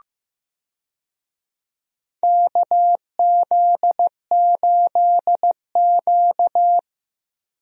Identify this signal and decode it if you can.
Morse code, heard as a keyed tone, about 15 wpm, 700 Hz. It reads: KZ8Q
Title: USA Call Signs - 4 Characters - Repeated 2x - 15wpm